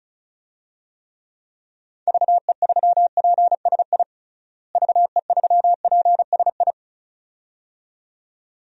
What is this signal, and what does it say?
Morse code, heard as a keyed tone, about 34 wpm, 700 Hz. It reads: VE3PSI VE3PSI
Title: DXCC-100 Call Signs - 6 Characters - Repeated 2x - 35wpm